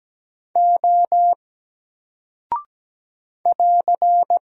Morse code, heard as a keyed tone, about 17 wpm, 700 Hz. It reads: O <AR>
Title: Single Character (Letter:Number:Punctuation:Prosign) 17wpm